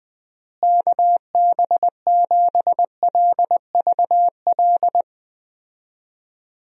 Morse code, heard as a keyed tone, about 20 wpm, 700 Hz. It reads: KB7LVL
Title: USA Call Signs - 6 Characters - Repeated 3x - 20wpm